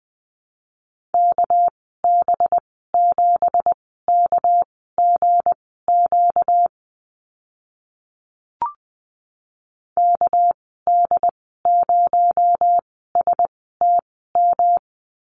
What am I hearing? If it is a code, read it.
Morse code, heard as a keyed tone, about 19 wpm, spaced as for 14 wpm, 700 Hz. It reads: KB7KGQ KD0STM
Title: 245 Lesson 39g - Letter z - Call Signs 20wpm